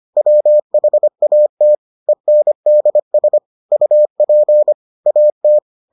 Morse code, heard as a keyed tone, about 25 wpm, 600 Hz. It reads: WHAT ENDS UP AT